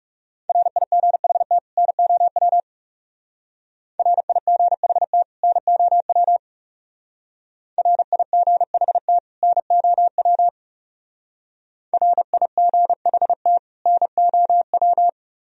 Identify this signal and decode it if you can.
Morse code, heard as a keyed tone, about 35 wpm, 700 Hz. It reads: RIGHT NOW RIGHT NOW RIGHT NOW RIGHT NOW
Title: Speed-Racing - Sets of 2 Words 30wpm